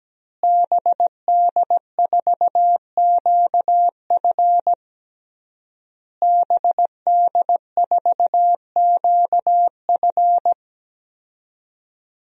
Morse code, heard as a keyed tone, about 17 wpm, 700 Hz. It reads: BD4QF BD4QF
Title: DXCC-100 Call Signs - 5 Characters - Repeated 3x - 17wpm